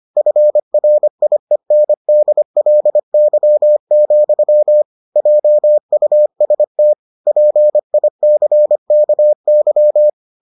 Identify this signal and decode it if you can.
Morse code, heard as a keyed tone, about 25 wpm, 600 Hz. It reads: FRIENDLY, JUST PICKY